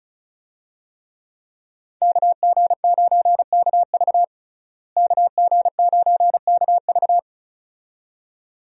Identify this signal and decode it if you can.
Morse code, heard as a keyed tone, about 36 wpm, 700 Hz. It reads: KG9KV KG9KV
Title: USA Call Signs - 5 Characters - Repeated 2x - 35wpm